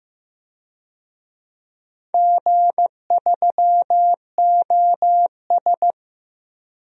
Morse code, heard as a keyed tone, about 15 wpm, 700 Hz. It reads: G3OS
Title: DXCC-100 Call Signs - 4 Characters - Repeated 2x - 15wpm